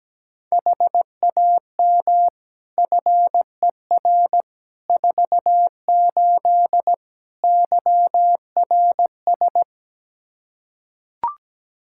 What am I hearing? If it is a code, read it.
Morse code, heard as a keyed tone, about 17 wpm, 700 Hz. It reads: HAM FER 48 YRS
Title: K1USN SST QSO Practice 17wpm